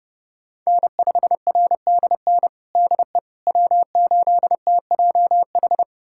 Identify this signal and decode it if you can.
Morse code, heard as a keyed tone, about 30 wpm, 700 Hz. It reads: N5RDN DE W8TJH
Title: USA Callsign DE Callsign - 5 Characters - 30wpm